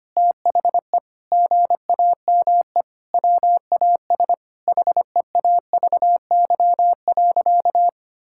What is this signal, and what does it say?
Morse code, heard as a keyed tone, about 25 wpm, 700 Hz. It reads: THE GAME WAS HEAVY.
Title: Sentences from Top 1600 Words 25wpm